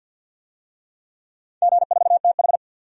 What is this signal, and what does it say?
Morse code, heard as a keyed tone, about 51 wpm, 700 Hz. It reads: G4TH
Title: DXCC-100 Call Signs - 4 Characters - 50wpm